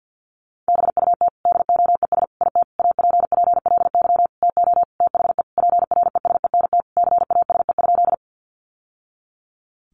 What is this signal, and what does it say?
Morse code, heard as a keyed tone, about 50 wpm, 700 Hz. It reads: BUT DOES IT APPLY TO THE PRESENT CASE?
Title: Sentences from Top 1200 Words 50wpm